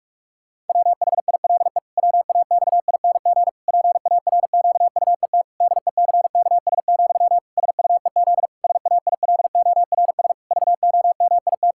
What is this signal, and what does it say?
Morse code, heard as a keyed tone, about 46 wpm, 700 Hz. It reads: WHILE WAXING PARQUET DECKS, SUEZ SAILORS VOMIT
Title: Pangram Challenge v2 - 45wpm